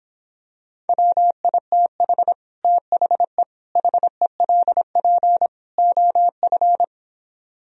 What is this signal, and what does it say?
Morse code, heard as a keyed tone, about 26 wpm, 700 Hz. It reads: WITH THE HELP OF